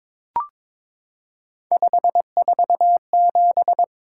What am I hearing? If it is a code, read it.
Morse code, heard as a keyed tone, about 22 wpm, 700 Hz. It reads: 547
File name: practice / 22wpm / Three Letter-Number Combinations - 22wpm